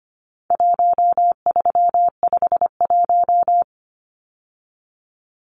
Morse code, heard as a keyed tone, about 25 wpm, 700 Hz. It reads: 1351
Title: Numbers - 4-Digit 25wpm